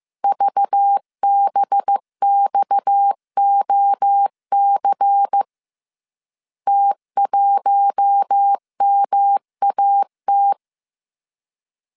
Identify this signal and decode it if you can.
Morse code, heard as a keyed tone, about 15 wpm, 800 Hz. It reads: VBXOC T1MAT